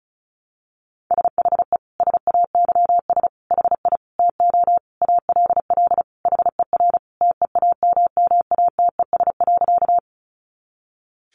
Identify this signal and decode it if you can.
Morse code, heard as a keyed tone, about 34 wpm, 700 Hz. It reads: SHE SAYS HI TO ALL HER TEAMMATES.
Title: Easy Phrases 35wpm